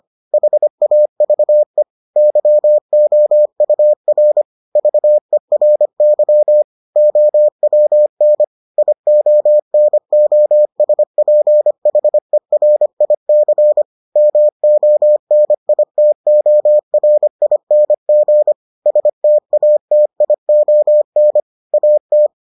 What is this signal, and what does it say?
Morse code, heard as a keyed tone, about 25 wpm, 600 Hz. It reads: HAVE YOUR VERY OWN IONOSPHERIC MONITORING STATION AT